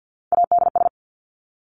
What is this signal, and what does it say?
Morse code, heard as a keyed tone, about 50 wpm, 700 Hz. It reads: ADS
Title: Abbreviations 50wpm